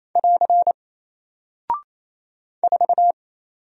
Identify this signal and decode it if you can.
Morse code, heard as a keyed tone, about 27 wpm, 700 Hz. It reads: <AR> 4
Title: Single Character (Letter:Number:Punctuation:Prosign) Alt 28wpm